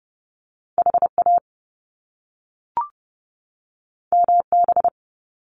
Morse code, heard as a keyed tone, about 30 wpm, 700 Hz. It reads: HA MB
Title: Top Two Letter Combinations in Words 30wpm